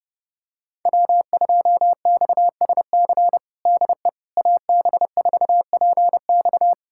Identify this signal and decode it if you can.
Morse code, heard as a keyed tone, about 30 wpm, 700 Hz. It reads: W2XSC DE AB4PX
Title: USA Callsign DE Callsign - 5 Characters - 30wpm